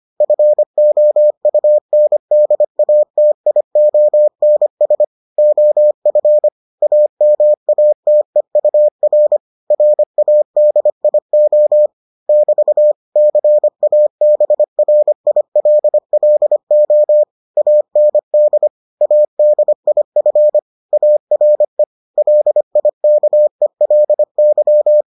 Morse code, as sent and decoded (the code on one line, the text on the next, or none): ..-. --- ..- -. -.. .- - .. --- -. ... / --- ..-. / .- -- .- - . ..- .-. / .-. .- -.. .. --- / -...- / -.-. .- -... .-. .. .-.. .-.. --- / .- -. -.. / .- -.. .. ..-. / .- .-. / .-.. .. -.- . .-.. -.--
FOUNDATIONS OF AMATEUR RADIO <BT> CABRILLO AND ADIF ARE LIKELY